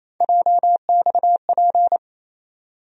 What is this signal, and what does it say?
Morse code, heard as a keyed tone, about 28 wpm, 700 Hz. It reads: JXP
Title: USA Call Sign Postfixes - 3 Letters 28wpm